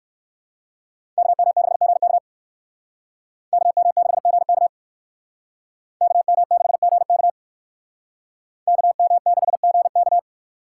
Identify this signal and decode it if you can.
Morse code, heard as a keyed tone, about 55 wpm, 700 Hz. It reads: KM6GK KM6GK KM6GK KM6GK
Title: Speed Racing - USA 2x2 Call Signs 45wpm